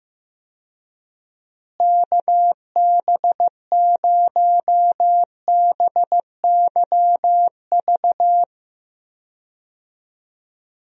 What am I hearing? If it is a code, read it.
Morse code, heard as a keyed tone, about 15 wpm, 700 Hz. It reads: KB0BYV
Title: USA Call Signs - 6 Characters - Repeated 3x - 15wpm